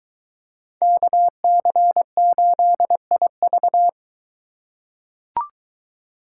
Morse code, heard as a keyed tone, about 22 wpm, 700 Hz. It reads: KC8IV